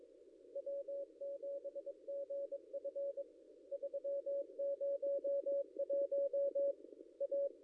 Morse code, heard as a keyed tone, about 23 wpm, 560 Hz. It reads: W7GF 301 A
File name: contests / ss-cw-k5tr-op-n5rz -0800